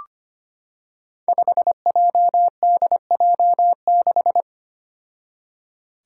Morse code, heard as a keyed tone, about 25 wpm, 700 Hz. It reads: 5JDJ6
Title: Five Challenging Letter-Number Combinations 25wpm